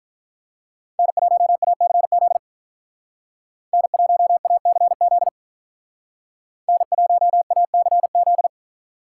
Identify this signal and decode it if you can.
Morse code, heard as a keyed tone, about 46 wpm, 700 Hz. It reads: N1ACZ N1ACZ N1ACZ